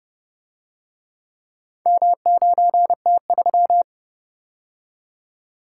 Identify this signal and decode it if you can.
Morse code, heard as a keyed tone, about 30 wpm, 700 Hz. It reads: M9T3